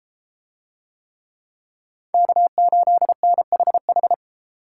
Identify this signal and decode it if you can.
Morse code, heard as a keyed tone, about 31 wpm, 700 Hz. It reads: K8NHH